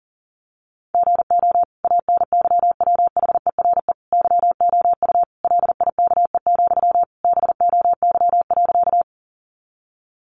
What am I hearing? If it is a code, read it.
Morse code, heard as a keyed tone, about 40 wpm, 700 Hz. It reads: GO ANYWHERE YOU LIKE, BOY.
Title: Sentences from Top 1400 Words 40wpm